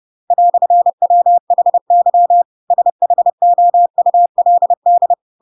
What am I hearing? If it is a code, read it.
Morse code, heard as a keyed tone, about 30 wpm, 700 Hz. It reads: "WHY SHOULD
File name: Pinocchio / Pinocchio ch8 30 30 ik1hge